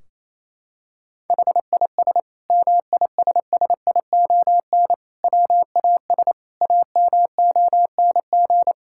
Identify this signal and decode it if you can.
Morse code, heard as a keyed tone, about 27 wpm, 700 Hz. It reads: HIS MISSION WAS AMONG
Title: Sentences from Top 1600 Words 28wpm